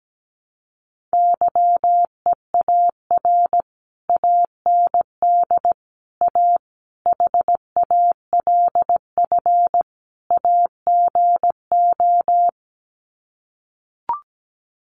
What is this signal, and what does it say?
Morse code, heard as a keyed tone, about 17 wpm, 700 Hz. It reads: YEAR AND A HALF AGO
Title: Sets of 5 Words Encore 17wpm